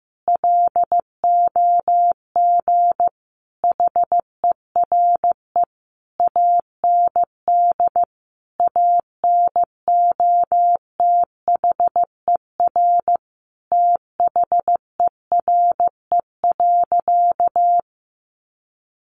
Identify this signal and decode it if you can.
Morse code, heard as a keyed tone, about 15 wpm, 700 Hz. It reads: LOG HERE AND ANOTHER THERE.
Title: Sentences from Top 1400 Words 15wpm